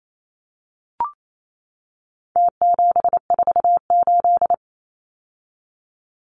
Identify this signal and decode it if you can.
Morse code, heard as a keyed tone, about 28 wpm, 700 Hz. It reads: T748
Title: Numbers - Contest Style - 4-Digit 28wpm